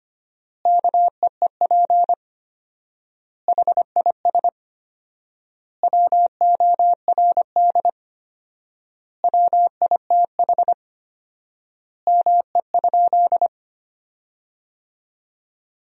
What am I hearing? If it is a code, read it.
Morse code, heard as a keyed tone, about 25 wpm, 700 Hz. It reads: KEEP HIS WORD WITH ME?
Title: 4x Word-Spacing - Sentences from Top 300 Words - 25wpm